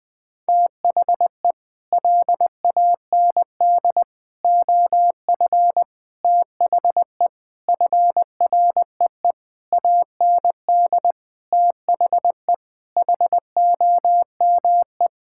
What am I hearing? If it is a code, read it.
Morse code, heard as a keyed tone, about 20 wpm, 700 Hz. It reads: THE LAND OF THE FREE AND THE HOME
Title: Sets of 8 Words - 20wpm